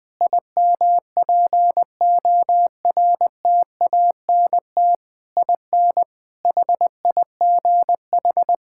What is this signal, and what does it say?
Morse code, heard as a keyed tone, about 20 wpm, 700 Hz. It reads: IMPORTANT IN HIGH
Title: Sentences from Top 1000 Words - Review Part 6 - 20wpm